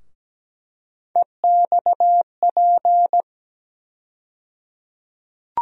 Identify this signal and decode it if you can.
Morse code, heard as a keyed tone, about 17 wpm, 700 Hz. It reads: EXP
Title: USA Call Sign Postfixes - 3 Letters 17wpm